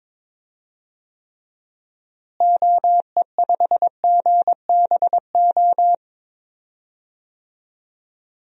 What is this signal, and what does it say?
Morse code, heard as a keyed tone, about 22 wpm, 700 Hz. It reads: OE5GBO
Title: DXCC-100 Call Signs - 6 Characters - Repeated 2x - 22wpm